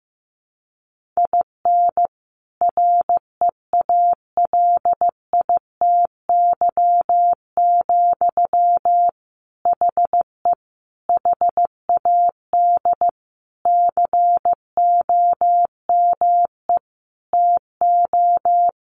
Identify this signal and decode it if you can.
Morse code, heard as a keyed tone, about 15 wpm, 700 Hz. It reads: IN REALITY, HE HAD COME TO
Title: Sentences from Top 1500 Words 15wpm